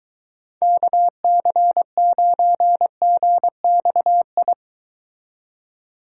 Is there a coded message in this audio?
Yes: Morse code from 0.6 to 4.5 s, heard as a keyed tone, about 23 wpm, 700 Hz. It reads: KC9GXI